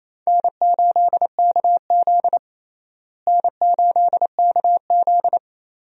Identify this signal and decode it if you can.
Morse code, heard as a keyed tone, about 27 wpm, 700 Hz. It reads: N8KZ N8KZ
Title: USA Call Signs - 4 Characters - Repeated 2x - 28wpm